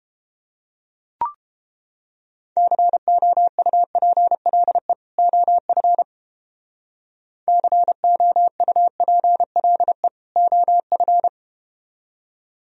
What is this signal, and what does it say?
Morse code, heard as a keyed tone, about 31 wpm, 700 Hz. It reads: COUPLE OF COUPLE OF